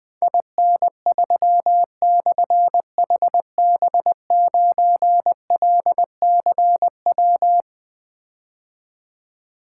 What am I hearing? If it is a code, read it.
Morse code, heard as a keyed tone, about 20 wpm, 700 Hz. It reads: IN3/HB9LCW
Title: Difficult Call Signs 20wpm